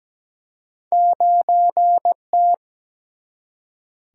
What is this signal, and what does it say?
Morse code, heard as a keyed tone, about 17 wpm, 700 Hz. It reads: 9T
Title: Single-Letter Single-Number 17wpm